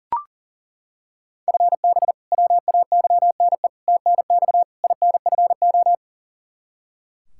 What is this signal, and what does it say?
Morse code, heard as a keyed tone, about 37 wpm, 700 Hz. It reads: FB WAYNE TNX INFO
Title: K1USN SST QSO Practice 40wpm